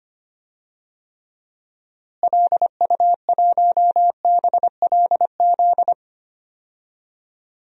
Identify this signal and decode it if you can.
Morse code, heard as a keyed tone, about 25 wpm, 700 Hz. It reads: LU1BLZ
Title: DXCC-100 Call Signs - 6 Characters - Repeated 3x - 25wpm